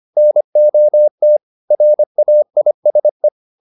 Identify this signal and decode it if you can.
Morse code, heard as a keyed tone, about 25 wpm, 600 Hz. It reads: NOT RAISE
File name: the old man and the sea part 1 25wpm